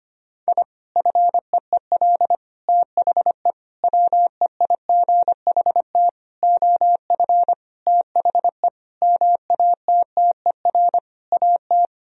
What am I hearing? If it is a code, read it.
Morse code, heard as a keyed tone, about 25 wpm, 700 Hz. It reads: I FEEL THE WEIGHT OF THE MATTER AT
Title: Sentences from Top 1000 Words - Review Part 3 - 25wpm